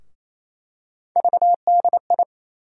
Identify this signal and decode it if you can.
Morse code, heard as a keyed tone, about 28 wpm, 700 Hz. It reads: VDI
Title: USA Call Sign Postfixes - 3 Letters 28wpm